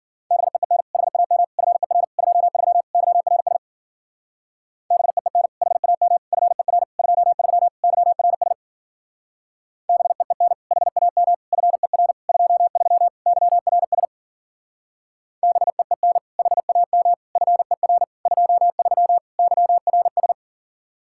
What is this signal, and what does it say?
Morse code, heard as a keyed tone, about 52 wpm, 700 Hz. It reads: BEEN HAM FER 23 YRS BEEN HAM FER 23 YRS BEEN HAM FER 23 YRS BEEN HAM FER 23 YRS